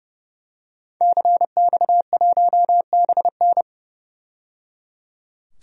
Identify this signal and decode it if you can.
Morse code, heard as a keyed tone, about 30 wpm, 700 Hz. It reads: CX1BN
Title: DXCC-100 Call Signs - 5 Characters - 30wpm